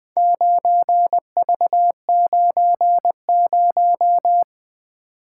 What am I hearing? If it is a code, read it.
Morse code, heard as a keyed tone, about 21 wpm, 700 Hz. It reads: 9V90
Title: Four Letter-Number Combinations - 20wpm